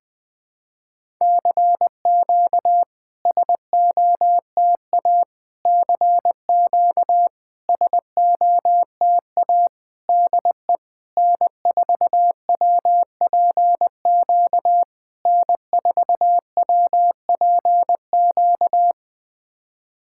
Morse code, heard as a keyed tone, about 20 wpm, 700 Hz. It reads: CQ SOTA CQ SOTA DE N4WPQ N4WPQ
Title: SOTA QSO Practice 20wpm